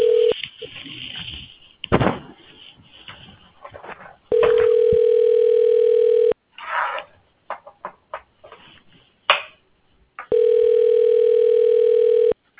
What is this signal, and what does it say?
Ringback tone (North America), 440 + 480 Hz.